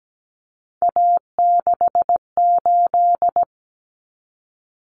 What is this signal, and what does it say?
Morse code, heard as a keyed tone, about 17 wpm, 700 Hz. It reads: A68
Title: Common Cut Numbers - 3-Digit 17wpm